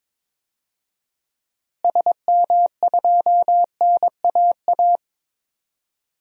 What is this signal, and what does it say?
Morse code, heard as a keyed tone, about 22 wpm, 700 Hz. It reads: SM2NAA
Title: DXCC-100 Call Signs - 6 Characters - Repeated 3x - 22wpm